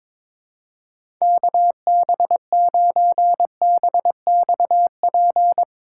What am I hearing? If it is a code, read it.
Morse code, heard as a keyed tone, about 22 wpm, 700 Hz. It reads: KB9BXP
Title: USA Call Signs - 6 Characters - 2x Character Spacing - 22wpm